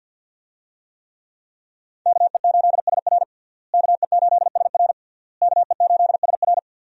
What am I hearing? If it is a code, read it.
Morse code, heard as a keyed tone, about 52 wpm, 700 Hz. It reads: KE8SR KE8SR KE8SR
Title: USA Call Signs - 5 Characters - Repeated 3x - 50wpm